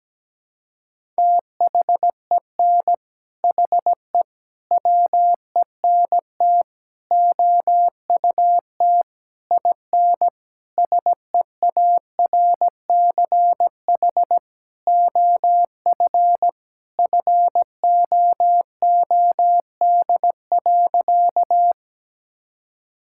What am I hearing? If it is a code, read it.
Morse code, heard as a keyed tone, about 17 wpm, 700 Hz. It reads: THEN HE WENT OUT IN SEARCH OF FOOD.
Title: Sentences from Top 1000 Words - Review Part 5 - 17wpm